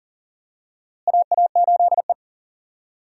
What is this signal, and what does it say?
Morse code, heard as a keyed tone, about 40 wpm, 700 Hz. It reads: AA8E
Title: USA Call Signs - 4 Characters - Repeated 2x - 40wpm